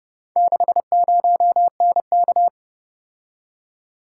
Morse code, heard as a keyed tone, about 28 wpm, 700 Hz. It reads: 60NK